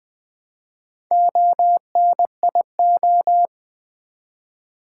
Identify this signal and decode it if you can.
Morse code, heard as a keyed tone, about 20 wpm, 700 Hz. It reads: ONIO
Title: Build A Word - 1 Letter At A Time - 3 to 5-Letter Words - 20wpm